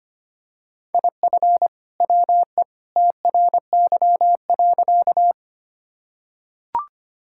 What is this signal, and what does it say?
Morse code, heard as a keyed tone, about 25 wpm, 700 Hz. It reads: IF WE TRY.
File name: practice / 25wpm / Sentences with 2-Word Phrases - 25wpm